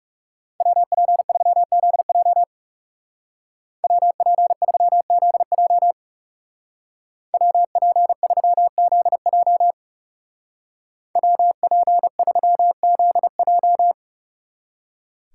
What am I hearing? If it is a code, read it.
Morse code, heard as a keyed tone, about 37 wpm, 700 Hz. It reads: WP3ZJ WP3ZJ WP3ZJ WP3ZJ
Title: Speed Racing - USA 2x2 Call Signs 30wpm